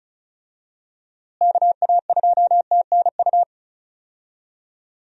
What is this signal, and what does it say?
Morse code, heard as a keyed tone, about 36 wpm, 700 Hz. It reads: KA2TNU
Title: USA Call Signs - 6 Characters - Repeated 2x - 35wpm